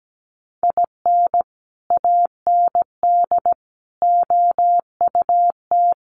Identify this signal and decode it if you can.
Morse code, heard as a keyed tone, about 17 wpm, 700 Hz. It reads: IN AND OUT
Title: Sets of 4 Words 17wpm